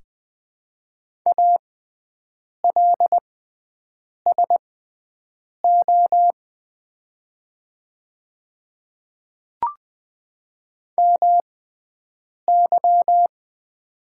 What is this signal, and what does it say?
Morse code, heard as a keyed tone, about 20 wpm, spaced as for 7 wpm, 690 Hz. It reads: ALSO MY
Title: 6x Character Spacing - Top 500 Words - 20wpm